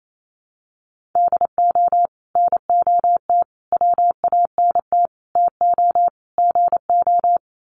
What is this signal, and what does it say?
Morse code, heard as a keyed tone, about 28 wpm, 700 Hz. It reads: DO NOT WANT TO GO
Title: Sets of 5 Words 28wpm